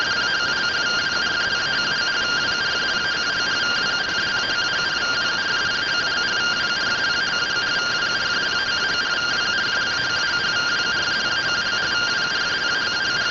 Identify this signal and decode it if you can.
Two-tone teleprinter signal, about 170 Hz shift.
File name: findings / AXIAL fr1 3mm.gen-12 rtty rx